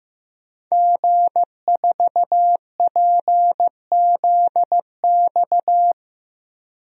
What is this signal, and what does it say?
Morse code, heard as a keyed tone, about 15 wpm, 700 Hz. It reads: G4PZX
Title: DXCC-100 Call Signs - 5 Characters - Repeated 3x - 15wpm